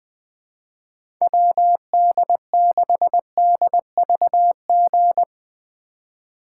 Morse code, heard as a keyed tone, about 21 wpm, 700 Hz. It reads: WD6DVG
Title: USA Call Signs - 3x Character Spacing - 20wpm